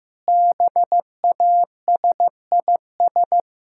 Morse code, heard as a keyed tone, about 16 wpm, 700 Hz. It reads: BASIS
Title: Sentences from Top 1000 Words - Review Part 5 - 15wpm